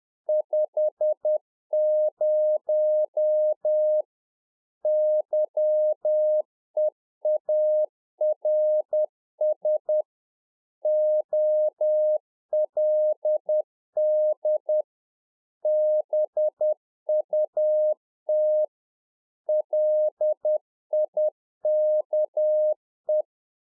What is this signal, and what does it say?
Morse code, heard as a keyed tone, about 10 wpm, 610 Hz. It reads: 50 YEARS OLD BUT LIKE
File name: Czech Army Morse Keys 10WPM